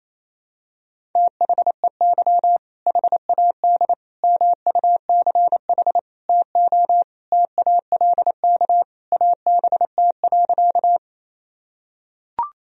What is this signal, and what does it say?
Morse code, heard as a keyed tone, about 27 wpm, 700 Hz. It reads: THEY HAD MUCH TO TALK ABT.
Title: Sentences from Top 400 Words and Abbreviations 28wpm